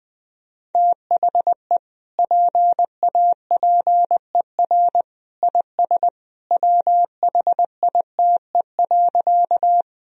Morse code, heard as a keyed tone, about 21 wpm, 700 Hz. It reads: THE PAPER IS WHITE.
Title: Sentences from Top 1000 Words - Review Part 1 - 20wpm